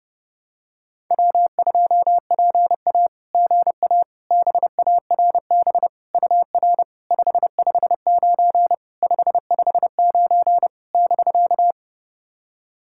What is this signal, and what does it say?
Morse code, heard as a keyed tone, about 30 wpm, 700 Hz. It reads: W2PA GA BARB UR 559 559 <BK>
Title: SOTA QSO Practice 30wpm